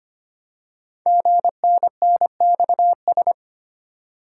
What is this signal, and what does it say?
Morse code, heard as a keyed tone, about 25 wpm, 700 Hz. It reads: GNNXS
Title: Code Groups - 5 Letters - 25wpm